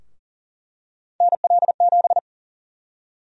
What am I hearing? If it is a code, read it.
Morse code, heard as a keyed tone, about 40 wpm, 700 Hz. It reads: NL7
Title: USA Call Sign Prefixes - 40wpm